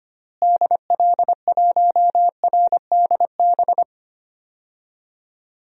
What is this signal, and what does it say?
Morse code, heard as a keyed tone, about 25 wpm, 700 Hz. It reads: DL1RDB
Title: DXCC-100 Call Signs - 6 Characters - Repeated 3x - 25wpm